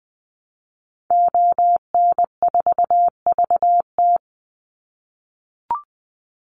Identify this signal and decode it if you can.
Morse code, heard as a keyed tone, about 20 wpm, 700 Hz. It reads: ON4VT
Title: Call Signs - 20wpm